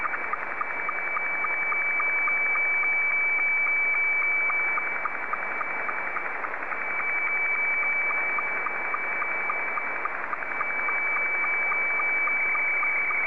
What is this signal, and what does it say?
SSTV (slow-scan television), Scottie 2.
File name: sstv / S2-ZL2TJM